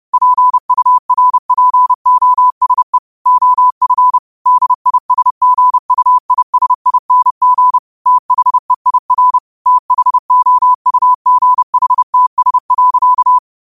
Morse code, heard as a keyed tone, about 30 wpm, 1000 Hz. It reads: PURPOSE OF DISGUISING THEIR THOUGHTS.